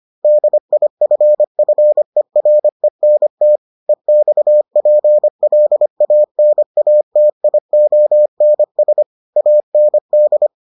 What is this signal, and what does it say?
Morse code, heard as a keyed tone, about 25 wpm, 600 Hz. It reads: DIFFERENT EXPLANATIONS AND